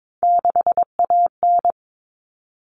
Morse code, heard as a keyed tone, about 22 wpm, 700 Hz. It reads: 6AN
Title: All Cut Numbers - 3-Digit 22wpm